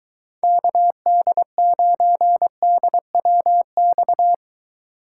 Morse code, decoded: KD9DWX

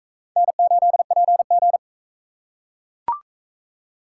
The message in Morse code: N8PG

